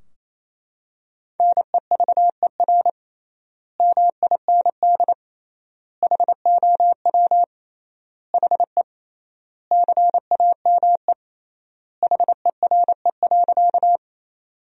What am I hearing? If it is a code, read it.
Morse code, heard as a keyed tone, about 28 wpm, 700 Hz. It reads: NEVER MIND HOW HE CAME HERE.
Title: 3x Word-Spacing - Sentences from Top 500 Words - 28wpm